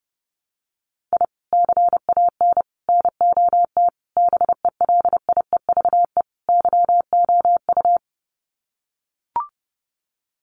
Morse code, heard as a keyed tone, about 30 wpm, 700 Hz. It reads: I CAN NOT BELIEVE YOU
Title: Sets of 5 Words Encore 30wpm